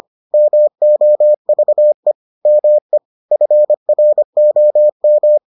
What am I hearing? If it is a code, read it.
Morse code, heard as a keyed tone, about 25 wpm, 600 Hz. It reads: MOVE ME FROM